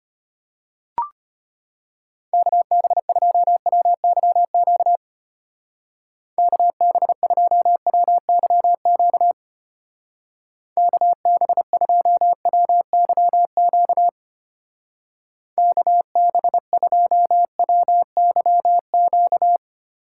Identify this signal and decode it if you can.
Morse code, heard as a keyed tone, about 31 wpm, 700 Hz. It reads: KB2WYQ KB2WYQ KB2WYQ KB2WYQ
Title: Speed Racing - USA 2x3 Call Signs 25wpm